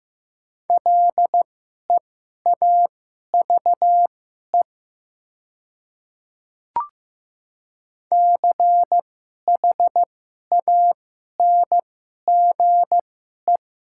Morse code, heard as a keyed tone, about 15 wpm, spaced as for 11 wpm, 700 Hz. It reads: LEAVE CHANGE
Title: 2x Character Spacing - Top 500 Words - 15wpm